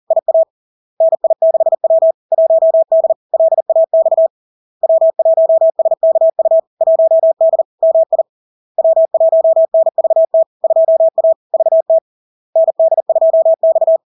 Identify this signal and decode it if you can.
Morse code, heard as a keyed tone, about 40 wpm, 650 Hz. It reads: IA NI6W 1D LAX W1SKU 1D MI W1NVT 2A VT ND2X
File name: LARK-FD-2022-Log-in-CW-40wpm-650-Hz